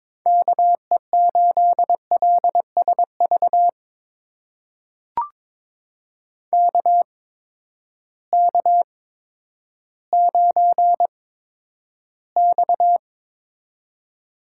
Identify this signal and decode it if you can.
Morse code, heard as a keyed tone, about 22 wpm, spaced as for 11 wpm, 700 Hz. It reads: KE8LSV KK9X
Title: USA Call Signs - 8x Character Spacing - 22wpm